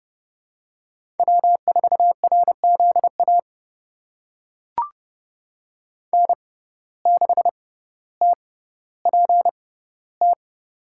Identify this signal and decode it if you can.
Morse code, heard as a keyed tone, about 30 wpm, spaced as for 15 wpm, 700 Hz. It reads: W4RZA N6TPT
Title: USA Call Signs - 5 Characters - 6x Character Spacing - 30wpm